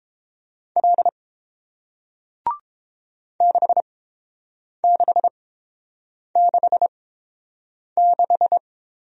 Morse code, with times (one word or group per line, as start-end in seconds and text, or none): 0.8-1.1 L
3.4-8.6 6666